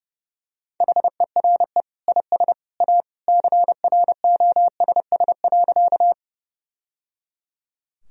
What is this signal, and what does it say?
Morse code, heard as a keyed tone, about 30 wpm, 700 Hz. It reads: HERE IS A CROSS.
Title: Sentences from Top 1000 Words - Review Part 1 - 30wpm